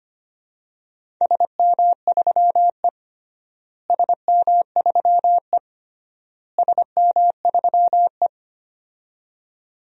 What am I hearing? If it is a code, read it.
Morse code, heard as a keyed tone, about 25 wpm, 700 Hz. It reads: SM3E SM3E SM3E
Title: DXCC-100 Call Signs - 4 Characters - Repeated 3x - 25wpm